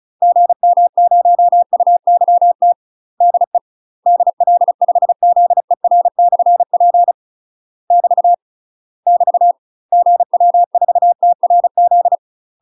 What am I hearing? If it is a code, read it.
Morse code, heard as a keyed tone, about 34 wpm, 700 Hz. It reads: GM0UYT DE DL5ZER/P <BT> <BT> GW4TRZ
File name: UK MASTER QSOs TRAINING / 35WPM / UK MASTER QSOs 0026